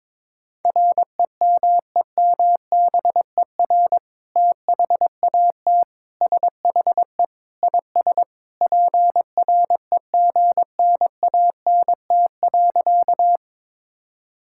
Morse code, heard as a keyed tone, about 22 wpm, 700 Hz. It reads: REMEMBER THAT SHE IS PREGNANT.
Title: Sentences from Top 1000 Words - Review Part 6 - 22wpm